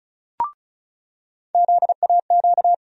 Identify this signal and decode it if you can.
Morse code, heard as a keyed tone, about 36 wpm, 700 Hz. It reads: ZAQ
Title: USA Call Sign Postfixes - 3 Letters 35wpm